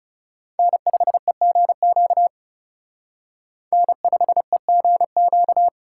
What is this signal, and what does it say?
Morse code, heard as a keyed tone, about 38 wpm, 700 Hz. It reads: N5EGQ N5EGQ